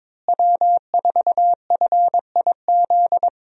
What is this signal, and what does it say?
Morse code, heard as a keyed tone, about 22 wpm, 700 Hz. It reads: W4FIZ
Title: USA Call Signs - 5 Characters - Repeated 2x - 22wpm